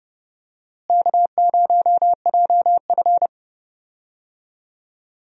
Morse code, heard as a keyed tone, about 30 wpm, 700 Hz. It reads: K0JF
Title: USA Call Signs - 4 Characters - Repeated 2x - 30wpm